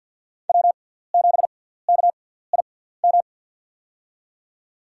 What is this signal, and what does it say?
Morse code, heard as a keyed tone, about 52 wpm, spaced as for 17 wpm, 700 Hz. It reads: W7KIM
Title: USA Call Signs - 5 Characters - 6x Character Spacing - 50wpm